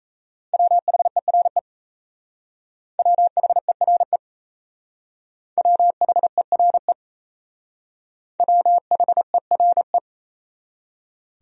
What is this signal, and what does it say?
Morse code, heard as a keyed tone, about 34 wpm, 700 Hz. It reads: WHERE WHERE WHERE WHERE
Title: Speed-Racing Top 500 Words 28wpm